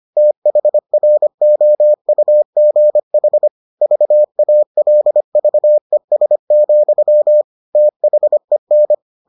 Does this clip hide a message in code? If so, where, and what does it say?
Yes: Morse code from 0.2 to 8.9 s, heard as a keyed tone, about 25 wpm, 600 Hz. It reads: THROUGH VALVES, THEN